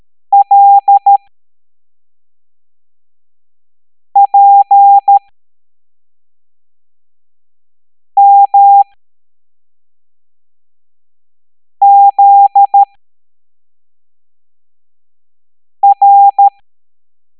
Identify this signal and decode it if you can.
Morse code, heard as a keyed tone, about 13 wpm, spaced as for 3 wpm, 800 Hz. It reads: LPMZR
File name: hamdata / L09-EX04